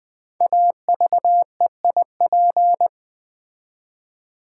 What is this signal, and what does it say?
Morse code, heard as a keyed tone, about 20 wpm, 700 Hz. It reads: AVEIP
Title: Code Groups - 5 Letters - 20wpm